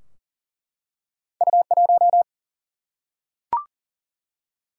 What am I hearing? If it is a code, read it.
Morse code, heard as a keyed tone, about 40 wpm, 700 Hz. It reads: U1
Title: Single-Letter Single-Number 40wpm